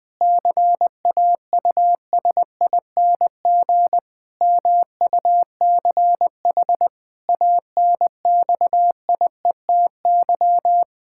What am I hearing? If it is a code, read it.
Morse code, heard as a keyed tone, about 20 wpm, 700 Hz. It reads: CAUSING MUCH ANXIETY
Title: Pangram Challenge v2 - 20wpm